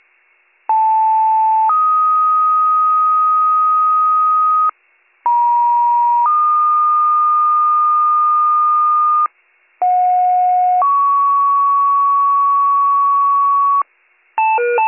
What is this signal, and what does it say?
Two-tone page, heard as a one-second tone, then a three-second tone, about 880 then 1250 Hz, about 950 then 1220 Hz, about 730 then 1120 Hz. It is heard from 0.7 to 13.8 s.